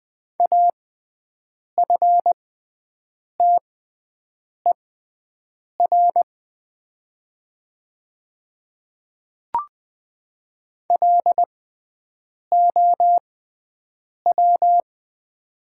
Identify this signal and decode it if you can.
Morse code, heard as a keyed tone, about 20 wpm, spaced as for 7 wpm, 690 Hz. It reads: AFTER LOW
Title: 6x Character Spacing - Top 300 Words - 20wpm